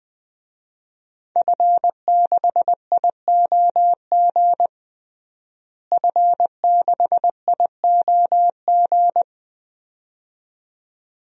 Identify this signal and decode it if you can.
Morse code, heard as a keyed tone, about 20 wpm, 700 Hz. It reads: F6IOG F6IOG
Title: DXCC-100 Call Signs - 5 Characters - Repeated 2x - 20wpm